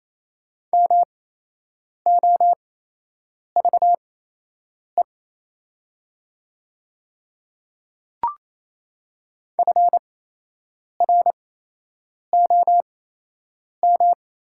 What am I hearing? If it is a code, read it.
Morse code, heard as a keyed tone, about 28 wpm, spaced as for 8 wpm, 700 Hz. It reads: MOVE FROM
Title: 8x Character Spacing - Top 500 Words - 28wpm